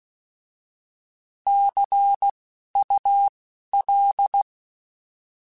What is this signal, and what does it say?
Morse code, heard as a keyed tone, about 16 wpm, spaced as for 11 wpm, 800 Hz. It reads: CUL
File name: Misc / Misc-mp3s / Words @10